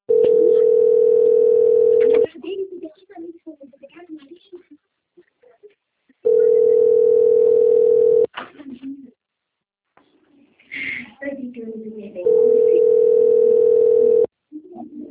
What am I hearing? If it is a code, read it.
Ringback tone (North America), 440 + 480 Hz, with speech.
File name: out- - - - - .3572